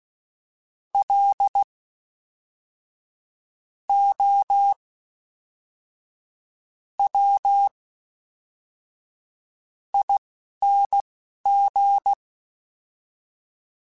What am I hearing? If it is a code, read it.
Morse code, heard as a keyed tone, about 16 wpm, spaced as for 11 wpm, 800 Hz. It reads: L O W ING
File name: Special INGs / Special INGs-mp3s / Fast ing @10